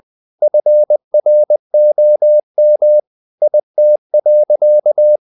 Morse code, heard as a keyed tone, about 20 wpm, 600 Hz. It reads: FROM IT.
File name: the old man and the sea / the old man and the sea part 1 20wpm